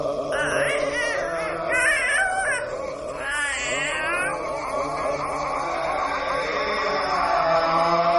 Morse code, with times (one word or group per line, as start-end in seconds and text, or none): none